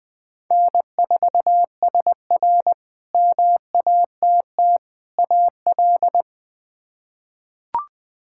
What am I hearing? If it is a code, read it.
Morse code, heard as a keyed tone, about 21 wpm, 700 Hz. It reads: N4SR MATT AL
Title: K1USN SST QSO Practice - 20wpm